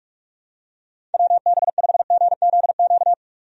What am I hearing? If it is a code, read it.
Morse code, heard as a keyed tone, about 46 wpm, 700 Hz. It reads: WB5GZQ
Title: USA Call Signs - 6 Characters - 45wpm